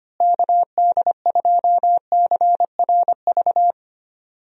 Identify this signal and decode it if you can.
Morse code, heard as a keyed tone, about 25 wpm, 700 Hz. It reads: KD2CRV
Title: USA Call Signs - 6 Characters - Repeated 2x - 25wpm